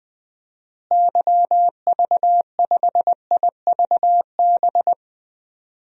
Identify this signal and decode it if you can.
Morse code, heard as a keyed tone, about 20 wpm, 700 Hz. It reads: YV5IVB
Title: DXCC-100 Call Signs - 6 Characters - Repeated 3x - 20wpm